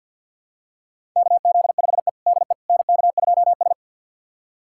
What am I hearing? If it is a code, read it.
Morse code, heard as a keyed tone, about 50 wpm, 700 Hz. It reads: KZ5E DE NK2S
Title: USA Callsign DE Callsign - 4 Characters - 50wpm